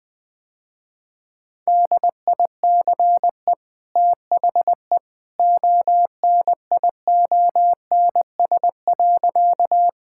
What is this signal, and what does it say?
Morse code, heard as a keyed tone, about 21 wpm, 700 Hz. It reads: DICE THE ONIONS.
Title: Easy Sentences with Hints - 20wpm